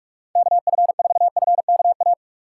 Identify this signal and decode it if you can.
Morse code, heard as a keyed tone, about 46 wpm, 700 Hz. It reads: KF4FKA
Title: USA Call Signs - 6 Characters - 3x Character Spacing - 45wpm